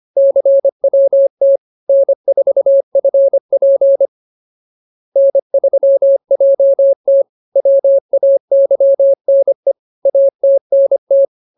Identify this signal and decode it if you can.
Morse code, heard as a keyed tone, about 25 wpm, 550 Hz. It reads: CWT N4FP N3JT WAYNE ATNT